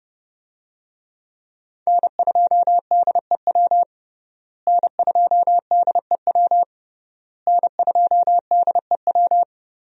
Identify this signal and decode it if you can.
Morse code, heard as a keyed tone, about 30 wpm, 700 Hz. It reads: N2DEW N2DEW N2DEW
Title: USA Call Signs - 5 Characters - Repeated 3x - 30wpm